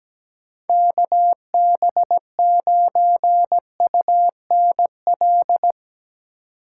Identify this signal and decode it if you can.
Morse code, heard as a keyed tone, about 17 wpm, 700 Hz. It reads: KB9UNL
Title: USA Call Signs - 6 Characters - 6x Character Spacing - 17wpm